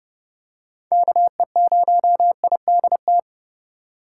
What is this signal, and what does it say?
Morse code, heard as a keyed tone, about 30 wpm, 700 Hz. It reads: KE0IDT